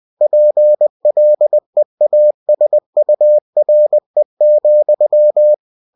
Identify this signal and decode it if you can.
Morse code, heard as a keyed tone, about 20 wpm, 600 Hz. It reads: PLEASURE,
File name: cw-books / winnie-the-pooh / winnie-20wpm-page-0107